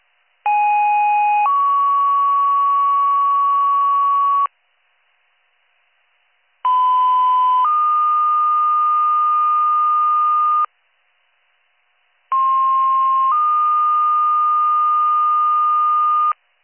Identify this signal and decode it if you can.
Two-tone page, heard as a one-second tone, then a three-second tone, about 830 then 1120 Hz, about 1010 then 1220 Hz, about 1060 then 1220 Hz. It is heard from 0.5 to 16.3 s.